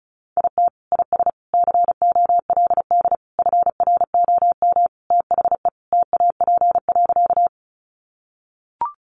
Morse code, heard as a keyed tone, about 34 wpm, 700 Hz. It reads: IT IS COLD FROM THE TAP.
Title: Sentences from Top 1400 Words 35wpm